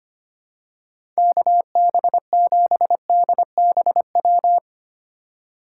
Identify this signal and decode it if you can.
Morse code, heard as a keyed tone, about 25 wpm, 700 Hz. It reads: KB7DBW
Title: USA Call Signs - 2x Character Spacing - 25wpm